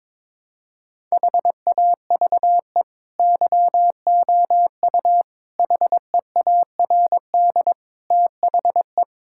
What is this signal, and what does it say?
Morse code, heard as a keyed tone, about 21 wpm, 700 Hz. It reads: HAVE YOU HEARD THE
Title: Sentences from Top 1000 Words - Review Part 5 - 22wpm